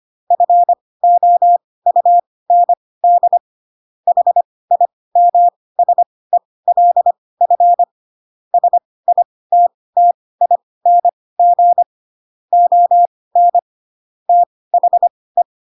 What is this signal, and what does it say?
Morse code, heard as a keyed tone, about 25 wpm, spaced as for 18 wpm, 700 Hz. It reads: FOUND HIMSELF SITTING ON THE
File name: Pinocchio / Pinocchio ch1 12 25 ik1hge